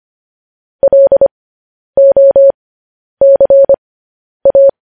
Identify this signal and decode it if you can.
Morse code, heard as a keyed tone, about 25 wpm, spaced as for 10 wpm, 550 Hz. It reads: LOCA
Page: news-headlines-in-morse-code-at-10-wpm